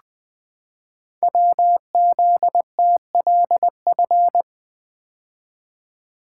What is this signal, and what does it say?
Morse code, heard as a keyed tone, about 20 wpm, 700 Hz. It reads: WZTLF
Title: Code Groups - 5 Letters - 20wpm